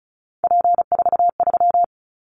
Morse code, heard as a keyed tone, about 36 wpm, 700 Hz. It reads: P43
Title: Callsign Prefixes for Top 100 DXCC - 35wpm